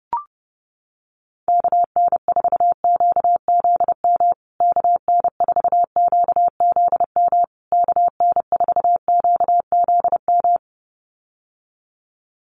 Morse code, heard as a keyed tone, about 30 wpm, 700 Hz. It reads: KN4QZM KN4QZM KN4QZM
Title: USA Call Signs Repeated 3x 30wpm